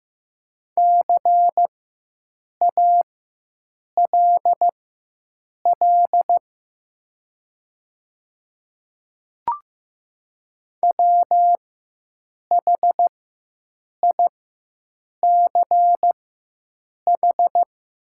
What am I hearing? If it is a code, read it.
Morse code, heard as a keyed tone, about 15 wpm, spaced as for 7 wpm, 700 Hz. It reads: CALL WHICH
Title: 4x Character Spacing - Top 100 Words - 15wpm